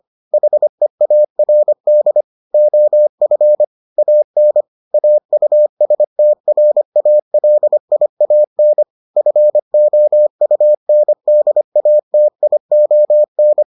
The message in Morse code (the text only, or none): HEARD OF AN AUSTRALIAN FOUNDATION